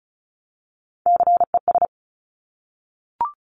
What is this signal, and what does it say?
Morse code, heard as a keyed tone, about 36 wpm, 700 Hz. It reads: CES
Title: Top Three Letter Combinations in Words 35wpm